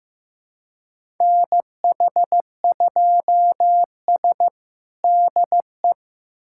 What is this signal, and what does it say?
Morse code, heard as a keyed tone, about 15 wpm, 700 Hz. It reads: NH2S DE
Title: USA Callsign DE Callsign - 4 Characters - 15wpm